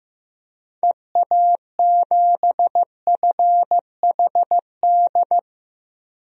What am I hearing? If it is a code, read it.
Morse code, heard as a keyed tone, about 15 wpm, 700 Hz. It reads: EA7FHD
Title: DXCC-100 Call Signs - 6 Characters - 15wpm